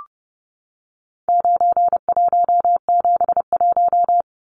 Morse code, heard as a keyed tone, about 29 wpm, 700 Hz. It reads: 9171
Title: Numbers - 4-Digit 30wpm